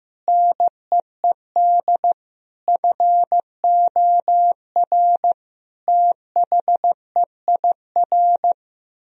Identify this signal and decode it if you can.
Morse code, heard as a keyed tone, about 15 wpm, 700 Hz. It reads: NEED FOR THEIR
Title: Sentences from Top 1000 Words - Review Part 4 - 15wpm